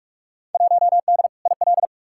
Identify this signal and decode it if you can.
Morse code, heard as a keyed tone, about 44 wpm, 700 Hz. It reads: 1D IL
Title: Field Day Exchanges - 45wpm